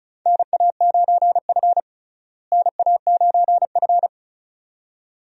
Morse code, heard as a keyed tone, about 34 wpm, 700 Hz. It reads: NA9F NA9F
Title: USA Call Signs - 4 Characters - Repeated 2x - 35wpm